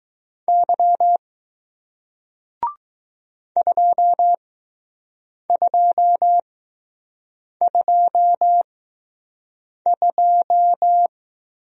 Morse code, heard as a keyed tone, about 20 wpm, spaced as for 6 wpm, 700 Hz. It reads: Y 2222